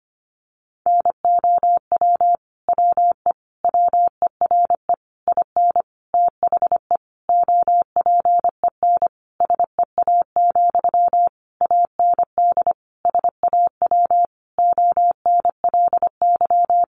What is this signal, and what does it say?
Morse code, heard as a keyed tone, about 25 wpm, 700 Hz. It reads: NOW WE WERE IN THE OPEN SEA, AND SAW ONLY
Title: Sentences from Top 1200 Words 25wpm